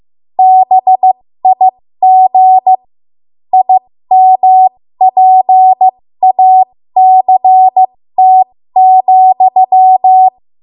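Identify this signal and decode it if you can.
Morse code, heard as a keyed tone, about 15 wpm, spaced as for 13 wpm, 750 Hz. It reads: BIG IMPACT,
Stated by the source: 13 wpm Morse code practice archive transmission.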